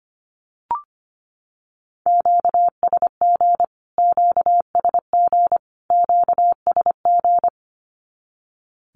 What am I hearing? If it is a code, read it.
Morse code, heard as a keyed tone, about 25 wpm, 700 Hz. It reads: QSG QSG QSG
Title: Q-Codes Repeated 3x 25wpm